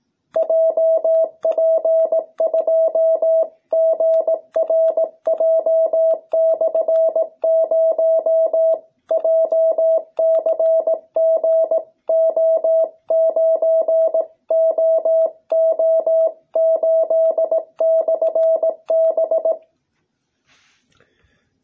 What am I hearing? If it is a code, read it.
Morse code, heard as a keyed tone, about 18 wpm, 660 Hz. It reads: JP2GRJ/0 JCGO9OO8/B